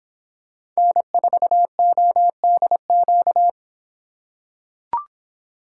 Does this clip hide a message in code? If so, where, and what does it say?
Yes: Morse code from 0.8 to 3.5 s, heard as a keyed tone, about 26 wpm, 700 Hz. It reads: N4ODQ